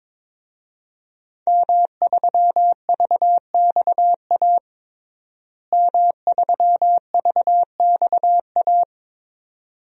Morse code, heard as a keyed tone, about 22 wpm, 700 Hz. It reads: M3VXA M3VXA
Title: DXCC-100 Call Signs - 5 Characters - Repeated 2x - 22wpm